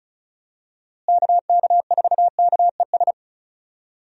Morse code, heard as a keyed tone, about 34 wpm, 700 Hz. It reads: KK4KES
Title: USA Call Signs - 6 Characters - Repeated 2x - 35wpm